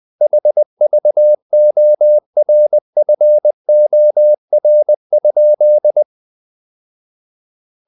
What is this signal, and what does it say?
Morse code, heard as a keyed tone, about 20 wpm, 600 Hz. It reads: HVORFOR?
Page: Ferie 20 wpm